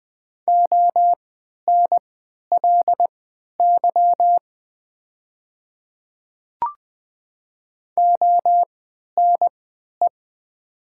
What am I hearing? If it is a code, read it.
Morse code, heard as a keyed tone, about 20 wpm, spaced as for 11 wpm, 700 Hz. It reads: ONLY ONE
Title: 3x Character Spacing - Top 200 Words - 20wpm